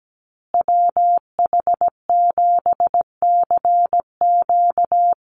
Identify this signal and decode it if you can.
Morse code, heard as a keyed tone, about 17 wpm, 700 Hz. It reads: WH7CQ
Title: USA Call Signs Repeated 2x 17wpm